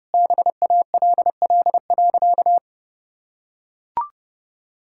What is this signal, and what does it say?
Morse code, heard as a keyed tone, about 30 wpm, 700 Hz. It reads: BALL.
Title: Sentences from Top 1000 Words - Review Part 6 - 30wpm